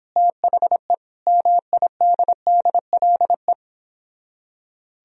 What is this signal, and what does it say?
Morse code, heard as a keyed tone, about 31 wpm, 700 Hz. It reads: THE MIDDLE